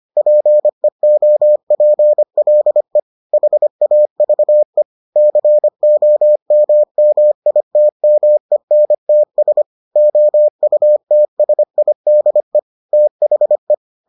Morse code, heard as a keyed tone, about 25 wpm, 600 Hz. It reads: PEOPLE HAVE COMMITMENTS OUTSIDE THE